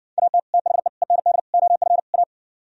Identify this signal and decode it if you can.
Morse code, heard as a keyed tone, about 56 wpm, 700 Hz. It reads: AT THE END OF A